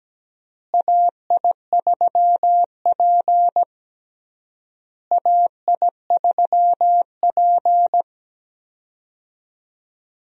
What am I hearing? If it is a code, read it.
Morse code, heard as a keyed tone, about 17 wpm, 700 Hz. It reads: AI3P AI3P
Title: USA Call Signs - 4 Characters - Repeated 3x - 17wpm